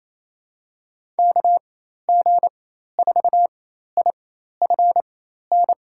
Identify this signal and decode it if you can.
Morse code, heard as a keyed tone, about 28 wpm, spaced as for 13 wpm, 700 Hz. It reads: KG4IFN
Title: USA Call Signs - 4x Character Spacing - 28wpm